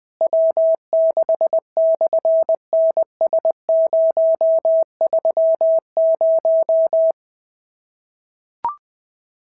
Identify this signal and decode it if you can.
Morse code, heard as a keyed tone, about 20 wpm, 690 Hz. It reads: W6/NS030
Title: SOTA QSO Practice 20wpm